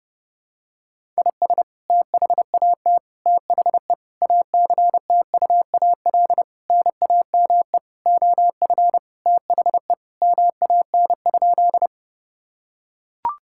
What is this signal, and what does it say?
Morse code, heard as a keyed tone, about 30 wpm, 700 Hz. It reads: IS THAT THE ACTUAL NAME OF THE MAN?
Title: Sentences from Top 1000 Words - Review Part 5 - 30wpm